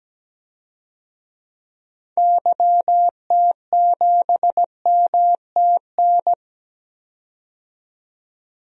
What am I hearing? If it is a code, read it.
Morse code, heard as a keyed tone, about 17 wpm, 700 Hz. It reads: YT7MTN
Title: DXCC-100 Call Signs - 6 Characters - 17wpm